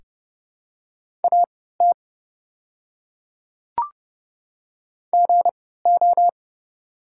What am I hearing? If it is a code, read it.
Morse code, heard as a keyed tone, about 30 wpm, spaced as for 17 wpm, 700 Hz. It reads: AT GO
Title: 3x Character Spacing - Top 100 Words - 30wpm